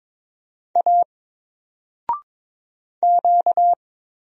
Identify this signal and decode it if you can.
Morse code, heard as a keyed tone, about 22 wpm, 700 Hz. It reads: A Q
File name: Single-Letter-Number-22wpm